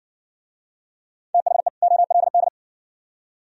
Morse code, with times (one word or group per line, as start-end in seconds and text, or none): 1.3-1.7 THE
1.8-2.5 OLD